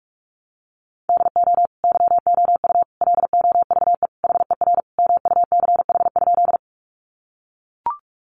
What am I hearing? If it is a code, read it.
Morse code, heard as a keyed tone, about 44 wpm, 700 Hz. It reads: DO YOU LOVE HER MUCH?
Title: Sentences from Top 400 Words 45wpm